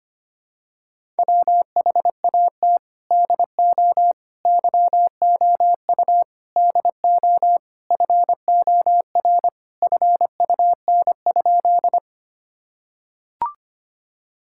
Morse code, heard as a keyed tone, about 25 wpm, 700 Hz. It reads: WHAT DO YOU DO FOR FUN?
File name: practice / 25wpm / Sentences with 4-Word Phrases - 25wpm